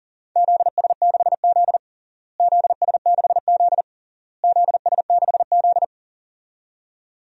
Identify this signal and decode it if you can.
Morse code, heard as a keyed tone, about 37 wpm, 700 Hz. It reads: ZS6Z ZS6Z ZS6Z
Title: DXCC-100 Call Signs - 4 Characters - Repeated 3x - 40wpm